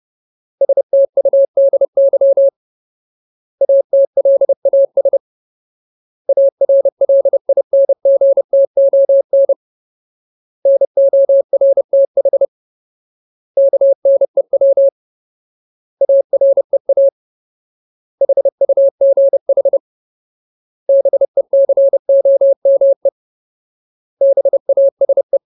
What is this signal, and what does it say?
Morse code, heard as a keyed tone, about 30 wpm, 550 Hz. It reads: STUDY ATLAS ARLINGTON NORTH KNEW AREA HUGH BECOME BASE
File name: Ultimate Word List / 30WPM Ultimate